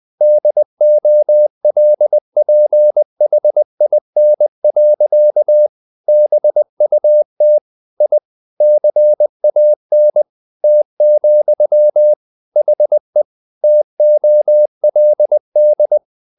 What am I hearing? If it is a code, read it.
Morse code, heard as a keyed tone, about 20 wpm, 600 Hz. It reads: DOLPHIN. BUT I CAN T, HE TOLD
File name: the old man and the sea part 2 20wpm